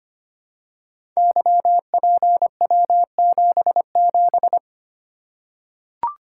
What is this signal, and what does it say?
Morse code, heard as a keyed tone, about 25 wpm, 700 Hz. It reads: YPW77
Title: Code Groups - 5 Letter:Numbers - 25wpm